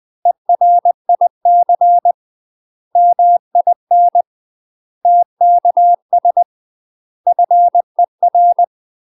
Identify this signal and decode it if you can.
Morse code, heard as a keyed tone, about 20 wpm, 700 Hz. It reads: ERIC MIN TKS FER
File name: UK MASTER QSOs TRAINING / 20WPM / UK MASTER QSOs 0020